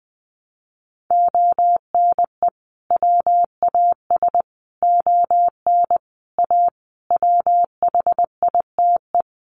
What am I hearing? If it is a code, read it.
Morse code, heard as a keyed tone, about 20 wpm, 700 Hz. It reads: ONE WAS ON A WHITE
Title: Sentences from Top 300 Words 20wpm